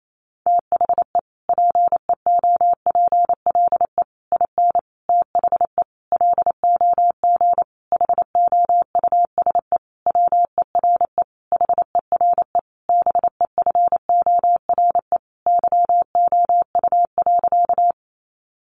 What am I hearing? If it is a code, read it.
Morse code, heard as a keyed tone, about 28 wpm, 700 Hz. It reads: THE PEOPLE IN THE LOG HOUSE WERE HERE BEFORE YOU.
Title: Sentences from Top 1400 Words 28wpm